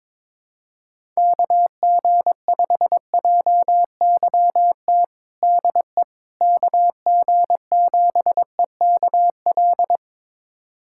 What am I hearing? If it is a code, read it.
Morse code, heard as a keyed tone, about 21 wpm, 700 Hz. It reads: KG5JYT DE KG7EKL
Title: USA Callsign DE Callsign 22wpm